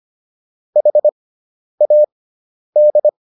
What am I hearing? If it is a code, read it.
Morse code, heard as a keyed tone, about 25 wpm, spaced as for 10 wpm, 600 Hz. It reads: HAD